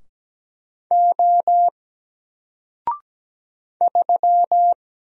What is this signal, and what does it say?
Morse code, heard as a keyed tone, about 17 wpm, 700 Hz. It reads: O 3
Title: Single Letter-Number 17wpm